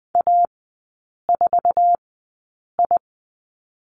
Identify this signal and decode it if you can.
Morse code, heard as a keyed tone, about 20 wpm, spaced as for 8 wpm, 700 Hz. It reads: A4I
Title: 107 Lesson 18e - Letter m - Three Characters 20wpm